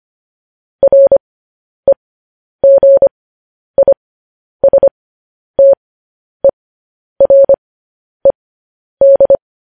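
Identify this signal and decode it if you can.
Morse code, heard as a keyed tone, about 25 wpm, spaced as for 10 wpm, 550 Hz. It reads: REGISTERED
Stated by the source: Morse code news headlines practice transmission.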